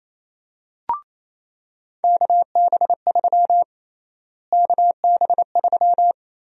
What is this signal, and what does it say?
Morse code, heard as a keyed tone, about 28 wpm, 700 Hz. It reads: KB3 KB3
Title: USA Call Sign Prefixes - Repeated 2x - 28wpm